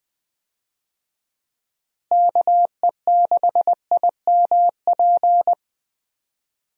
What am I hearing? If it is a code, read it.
Morse code, heard as a keyed tone, about 20 wpm, 700 Hz. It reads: KE6IMP
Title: USA Call Signs - 6 Characters - Repeated 3x - 20wpm